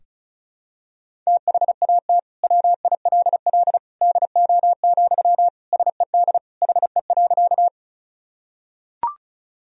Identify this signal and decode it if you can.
Morse code, heard as a keyed tone, about 34 wpm, 700 Hz. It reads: THAT WILL DO, SED HE.
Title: Sentences from Top 100 Words and Abbreviations 35wpm